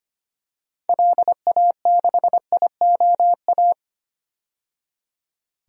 Morse code, heard as a keyed tone, about 25 wpm, 700 Hz. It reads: LA6IOA
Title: DXCC-100 Call Signs - 6 Characters - 25wpm